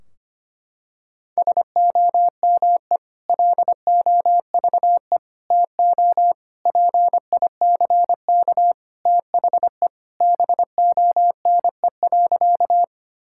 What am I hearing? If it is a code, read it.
Morse code, heard as a keyed tone, about 25 wpm, 700 Hz. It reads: SOME LOVE TO PICK THE BONE.
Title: Sentences from Top 1600 Words 25wpm